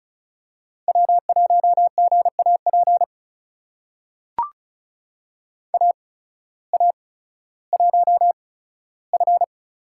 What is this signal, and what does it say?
Morse code, heard as a keyed tone, about 36 wpm, spaced as for 16 wpm, 700 Hz. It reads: W1GAP AA1F
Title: USA Call Signs - 5 Characters - 8x Character Spacing - 35wpm